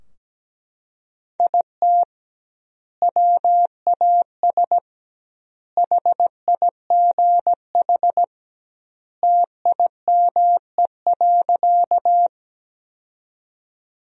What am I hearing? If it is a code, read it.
Morse code, heard as a keyed tone, about 17 wpm, 700 Hz. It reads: IT WAS HIGH TIME.
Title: 2x Word-Spacing - Sentences from Top 200 Words - 17wpm